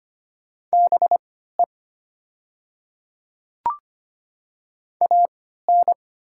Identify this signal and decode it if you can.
Morse code, heard as a keyed tone, about 25 wpm, spaced as for 14 wpm, 700 Hz. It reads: BE AN
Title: 3x Character Spacing - Top 100 Words - 25wpm